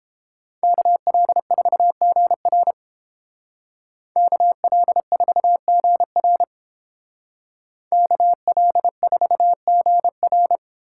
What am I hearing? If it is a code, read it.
Morse code, heard as a keyed tone, about 29 wpm, 700 Hz. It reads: KL4GR KL4GR KL4GR